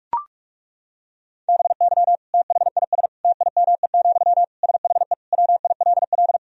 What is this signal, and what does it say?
Morse code, heard as a keyed tone, about 41 wpm, 700 Hz. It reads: BY THIS TIME, SHE WILL
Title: Sentences from Top 1000 Words - Review Part 4 - 45wpm